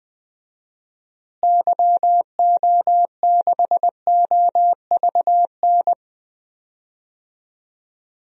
Morse code, heard as a keyed tone, about 20 wpm, 700 Hz. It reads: YO6OVN
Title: DXCC-100 Call Signs - 6 Characters - Repeated 2x - 20wpm